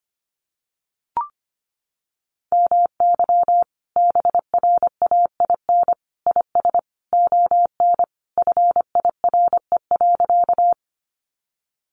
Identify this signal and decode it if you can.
Morse code, heard as a keyed tone, about 25 wpm, 700 Hz. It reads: MY BRAIN IS ON FIRE.
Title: Sentences from Top 1100 Words 25wpm